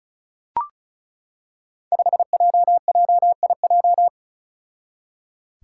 Morse code, heard as a keyed tone, about 36 wpm, 700 Hz. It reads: 5JJIJ
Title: Five Challenging Letter-Number Combinations 35wpm